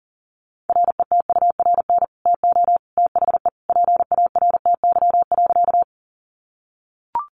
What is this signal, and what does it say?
Morse code, heard as a keyed tone, about 40 wpm, 700 Hz. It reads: RETURN TO THE PARTY.
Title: Sentences from Top 1400 Words 40wpm